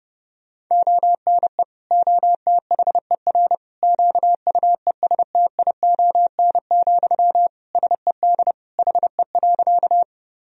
Morse code, heard as a keyed tone, about 30 wpm, 700 Hz. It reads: ONE OTHER QUESTION, SED HE.
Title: Sentences from Top 400 Words and Abbreviations 30wpm